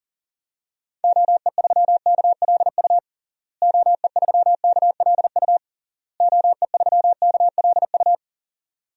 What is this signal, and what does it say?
Morse code, heard as a keyed tone, about 40 wpm, 700 Hz. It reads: OE3KLU OE3KLU OE3KLU
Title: DXCC-100 Call Signs - 6 Characters - Repeated 3x - 40wpm